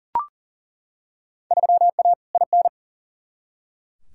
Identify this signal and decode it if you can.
Morse code, heard as a keyed tone, about 40 wpm, 700 Hz. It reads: 3A IN
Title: Field Day Exchanges Encore - 40wpm